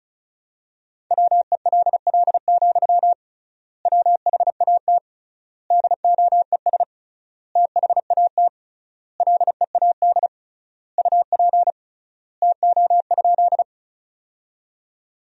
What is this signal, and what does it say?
Morse code, heard as a keyed tone, about 36 wpm, 700 Hz. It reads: WELL, WHAT DOES THAT LEAD UP TO?
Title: 3x Word-Spacing - Sentences from Top 500 Words - 35wpm